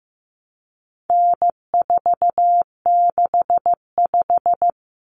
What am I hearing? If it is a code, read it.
Morse code, heard as a keyed tone, about 15 wpm, 700 Hz. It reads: N465
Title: Common Cut Numbers - 4-Digit 15wpm